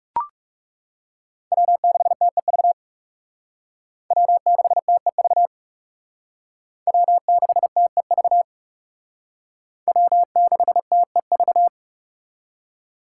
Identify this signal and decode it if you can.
Morse code, heard as a keyed tone, about 36 wpm, 700 Hz. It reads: W6TEV W6TEV W6TEV W6TEV
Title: Speed Racing - USA 1x3 Call Signs 30wpm